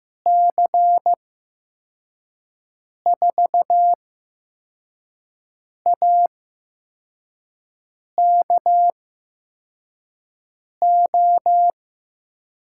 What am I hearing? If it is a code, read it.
Morse code, heard as a keyed tone, about 15 wpm, spaced as for 4 wpm, 700 Hz. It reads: C4AKO
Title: USA Call Signs - 6 Characters - 8x Character Spacing - 15wpm